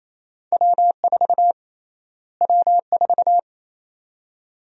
Morse code, heard as a keyed tone, about 28 wpm, 700 Hz. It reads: W4 W4
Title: USA Call Sign Prefixes - Repeated 2x - 28wpm